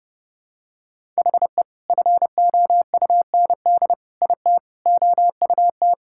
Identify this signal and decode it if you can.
Morse code, heard as a keyed tone, about 32 wpm, 700 Hz. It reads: HE FOUND IT OUT
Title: Sentences from Top 1600 Words 30wpm